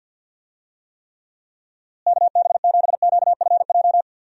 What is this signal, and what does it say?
Morse code, heard as a keyed tone, about 52 wpm, 700 Hz. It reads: KB7QFJ
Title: USA Call Signs - 6 Characters - Repeated 2x - 50wpm